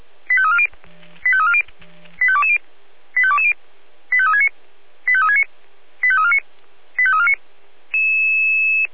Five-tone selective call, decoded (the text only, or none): 85340 85340 85220 85220 85369 85369 85349 85349